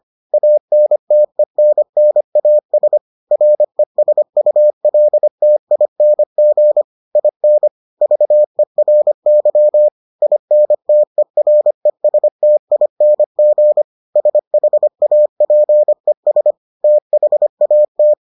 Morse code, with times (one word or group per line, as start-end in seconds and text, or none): 0.3-3.0 ANTENNAS
3.3-6.8 RESULTING
7.2-7.7 IN
8.0-9.9 VERY
10.2-13.8 INTERESTING
14.2-16.5 SHAPES
16.8-18.1 THAT